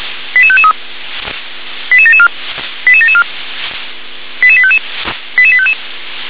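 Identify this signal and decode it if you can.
Five-tone selective call (ZVEI), heard as quick runs of five tones, tones of about 70 ms. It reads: 88553 88774 88774 88855 88855
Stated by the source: ZVEI five-tone selective call 88836 received by SDR.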